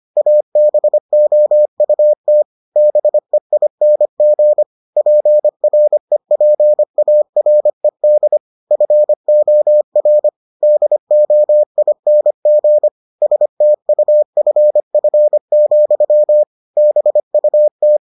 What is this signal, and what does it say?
Morse code, heard as a keyed tone, about 25 wpm, 600 Hz. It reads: ABOUT BEING PREPARED FOR DOING STUFF, BUT